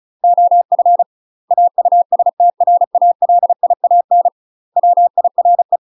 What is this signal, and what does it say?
Morse code, heard as a keyed tone, about 34 wpm, 700 Hz. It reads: OF AUSTRALIAN WIRE
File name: UK MASTER QSOs TRAINING / 35WPM / UK MASTER QSOs 0015